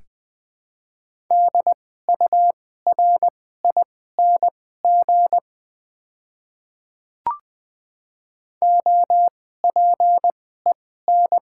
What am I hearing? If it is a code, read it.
Morse code, heard as a keyed tone, about 20 wpm, spaced as for 15 wpm, 700 Hz. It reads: DURING OPEN
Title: 2x Character Spacing - Top 500 Words - 20wpm